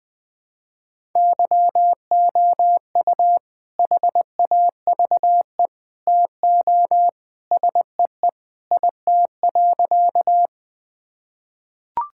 Morse code, heard as a keyed tone, about 20 wpm, 700 Hz. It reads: YOU HAVE TO SEE IT.
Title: Sentences with 2-Word Phrases - 20wpm